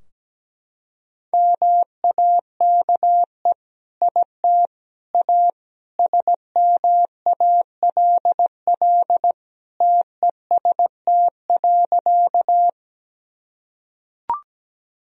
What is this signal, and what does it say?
Morse code, heard as a keyed tone, about 17 wpm, 700 Hz. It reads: MAKE IT A SMALL TEST.
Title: Sentences from Top 1000 Words - Review Part 3 - 17wpm